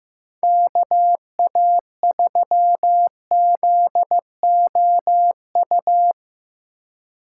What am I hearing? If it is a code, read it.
Morse code, heard as a keyed tone, about 15 wpm, 700 Hz. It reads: KA3ZOU
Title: USA Call Signs - 6 Characters - 3x Character Spacing - 15wpm